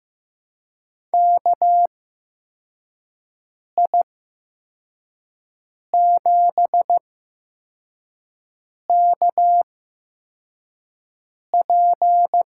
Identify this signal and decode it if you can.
Morse code, heard as a keyed tone, about 15 wpm, spaced as for 4 wpm, 700 Hz. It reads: KI7KP
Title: USA Call Signs - 6 Characters - 8x Character Spacing - 15wpm